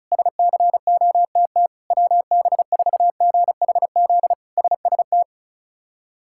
Morse code, heard as a keyed tone, about 36 wpm, 700 Hz. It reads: SCOTT WB4GHZ SST
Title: K1USN SST QSO Practice - 35wpm